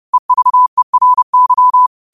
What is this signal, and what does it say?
Morse code, heard as a keyed tone, about 30 wpm, 1000 Hz. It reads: EVERY
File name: qotdfiles / 30wpm